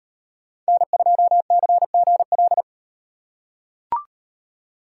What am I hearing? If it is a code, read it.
Morse code, heard as a keyed tone, about 36 wpm, 700 Hz. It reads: N2CGL